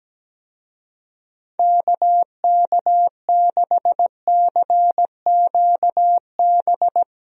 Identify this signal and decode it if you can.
Morse code, heard as a keyed tone, about 17 wpm, 700 Hz. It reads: KK6CQB
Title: USA Call Signs - 6 Characters - Repeated 2x - 17wpm